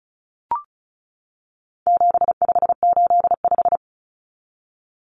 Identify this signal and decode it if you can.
Morse code, heard as a keyed tone, about 34 wpm, 700 Hz. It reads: 7585
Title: Common Cut Numbers - 4-Digit 35wpm